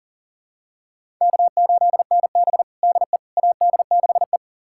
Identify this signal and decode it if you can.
Morse code, heard as a keyed tone, about 40 wpm, 700 Hz. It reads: K8NB DE AD6E
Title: USA Callsign DE Callsign - 4 Characters - 40wpm